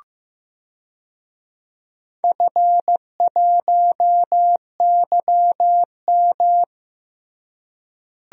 Morse code, heard as a keyed tone, about 15 wpm, 700 Hz. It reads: F1YM
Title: DXCC-100 Call Signs - 4 Characters - Repeated 3x - 15wpm